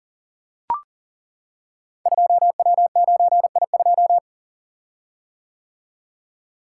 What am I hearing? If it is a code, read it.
Morse code, heard as a keyed tone, about 39 wpm, 700 Hz. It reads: 2W9I2
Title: Code Groups - 5 Letter:Numbers - 40wpm